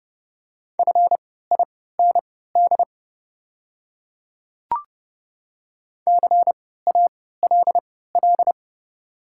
Morse code, heard as a keyed tone, about 32 wpm, spaced as for 17 wpm, 700 Hz. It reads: FIND CALL
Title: 3x Character Spacing - Top 100 Words - 30wpm